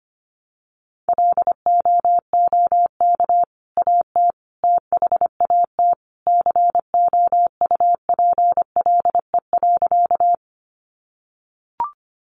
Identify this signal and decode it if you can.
Morse code, heard as a keyed tone, about 25 wpm, 700 Hz. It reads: LOOK AT THAT COUPLE.
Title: Sentences from Top 1100 Words 25wpm